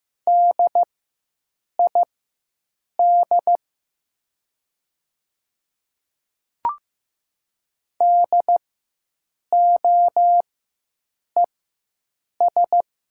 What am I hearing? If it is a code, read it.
Morse code, heard as a keyed tone, about 15 wpm, spaced as for 7 wpm, 700 Hz. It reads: DID DOES
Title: 4x Character Spacing - Top 500 Words - 15wpm